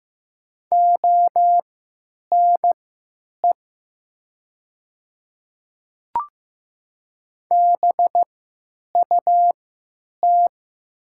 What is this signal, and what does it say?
Morse code, heard as a keyed tone, about 15 wpm, spaced as for 9 wpm, 700 Hz. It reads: ONE BUT
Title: 3x Character Spacing - Top 100 Words - 15wpm